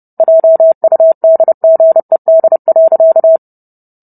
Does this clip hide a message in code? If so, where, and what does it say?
Yes: Morse code from 0.2 to 3.4 s, heard as a keyed tone, about 30 wpm, 650 Hz. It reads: JUDGED.